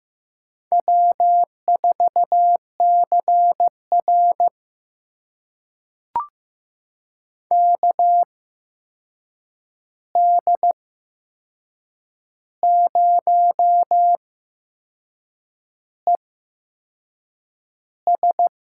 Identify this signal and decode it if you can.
Morse code, heard as a keyed tone, about 15 wpm, spaced as for 6 wpm, 700 Hz. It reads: W4CR KD0ES
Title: USA Call Signs - 8x Character Spacing - 15wpm